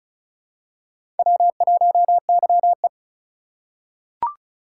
Morse code, heard as a keyed tone, about 34 wpm, 700 Hz. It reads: W1YE
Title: USA Call Signs - 4 Characters - 6x Character Spacing - 35wpm